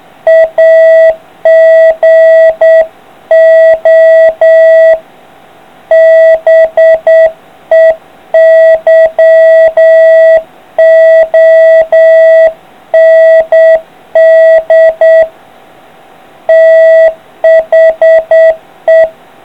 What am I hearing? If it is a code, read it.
Morse code, heard as a keyed tone, about 8 wpm, 650 Hz. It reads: AGO BEYOND THE